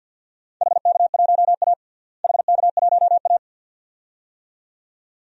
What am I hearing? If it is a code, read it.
Morse code, heard as a keyed tone, about 48 wpm, 700 Hz. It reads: HK1A HK1A
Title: DXCC-100 Call Signs - 4 Characters - Repeated 2x - 50wpm